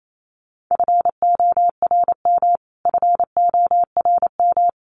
Morse code, heard as a keyed tone, about 28 wpm, 700 Hz. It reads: FORM FORM
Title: Top 200 Words Repeated 28wpm